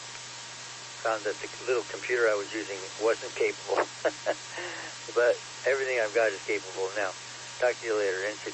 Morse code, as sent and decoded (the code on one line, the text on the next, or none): none